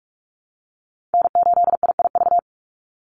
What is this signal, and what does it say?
Morse code, heard as a keyed tone, about 44 wpm, 700 Hz. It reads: N8IIV
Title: USA Call Signs 45wpm